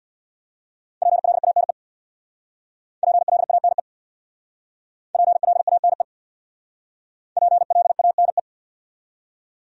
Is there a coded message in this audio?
Yes: Morse code from 1.0 to 8.4 s, heard as a keyed tone, about 67 wpm, 700 Hz. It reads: PLANE PLANE PLANE PLANE